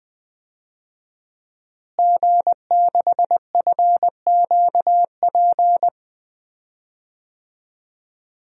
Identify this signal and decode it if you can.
Morse code, heard as a keyed tone, about 20 wpm, 700 Hz. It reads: G6FQP
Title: DXCC-100 Call Signs - 5 Characters - Repeated 3x - 20wpm